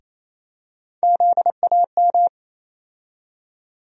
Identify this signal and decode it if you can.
Morse code, heard as a keyed tone, about 28 wpm, 700 Hz. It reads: ZAM
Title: USA Call Sign Postfixes - 3 Letters 28wpm